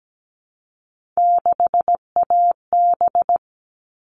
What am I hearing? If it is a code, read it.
Morse code, heard as a keyed tone, about 17 wpm, 700 Hz. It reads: 6AB
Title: All Cut Numbers - 3-Digit 17wpm